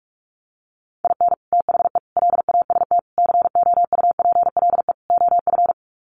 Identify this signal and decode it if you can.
Morse code, heard as a keyed tone, about 44 wpm, 700 Hz. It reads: IN THE LAST COUPLE OF
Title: Sets of 5 Words 45wpm